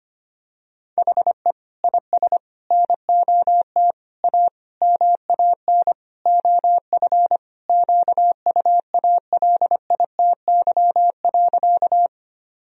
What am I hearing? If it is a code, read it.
Morse code, heard as a keyed tone, about 25 wpm, 700 Hz. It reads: HE IS NOT A MAN OF QUALITY.
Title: Sentences from Top 1000 Words - Review Part 4 - 25wpm